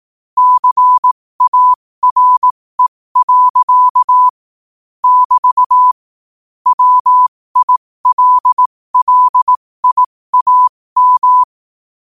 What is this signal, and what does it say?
Morse code, heard as a keyed tone, about 18 wpm, spaced as for 15 wpm, 1000 Hz. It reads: CARE. <BT> WILLIAM
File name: qotdfiles / 15wpm